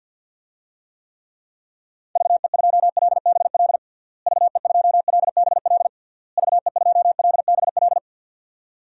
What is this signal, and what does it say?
Morse code, heard as a keyed tone, about 51 wpm, 700 Hz. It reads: VE2LBL VE2LBL VE2LBL
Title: DXCC-100 Call Signs - 6 Characters - Repeated 3x - 50wpm